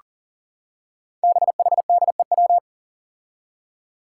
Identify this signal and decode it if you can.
Morse code, heard as a keyed tone, about 40 wpm, 700 Hz. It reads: BHDEW
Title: Five Challenging Letter-Number Combinations 40wpm